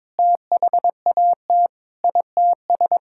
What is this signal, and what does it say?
Morse code, heard as a keyed tone, about 22 wpm, 700 Hz. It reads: THAT ITS
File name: Sentences with 7-Word Phrases - 22wpm